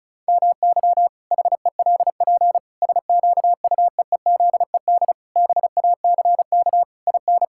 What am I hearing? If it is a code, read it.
Morse code, heard as a keyed tone, about 34 wpm, 700 Hz. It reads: MY HELP SQUEEZED BACK IN
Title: Pangram Challenge v2 - 35wpm